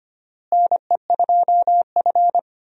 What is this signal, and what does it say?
Morse code, heard as a keyed tone, about 25 wpm, 700 Hz. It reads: NE2F
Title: USA Call Signs - 4 Characters - Repeated 3x - 25wpm